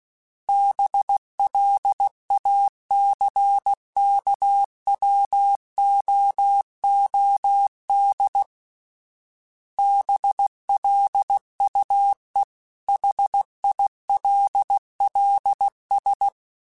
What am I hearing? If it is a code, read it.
Morse code, heard as a keyed tone, about 16 wpm, 800 Hz. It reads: BLACKWOOD BLUE HILLS
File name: fromSMs / FromSMs mp3s / Cities@15